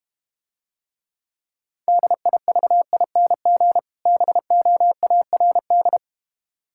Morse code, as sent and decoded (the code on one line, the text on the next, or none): -.. .. ...- .. -. --. / -... --- .- .-. -..
DIVING BOARD